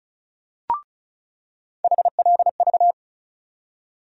Morse code, heard as a keyed tone, about 36 wpm, 700 Hz. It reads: HLV
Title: USA Call Sign Postfixes - 3 Letters 35wpm